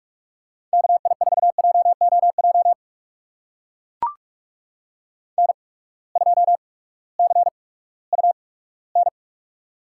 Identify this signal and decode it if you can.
Morse code, heard as a keyed tone, about 44 wpm, spaced as for 20 wpm, 700 Hz. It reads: KI4JOJ N2CUN
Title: USA Call Signs - 8x Character Spacing - 45wpm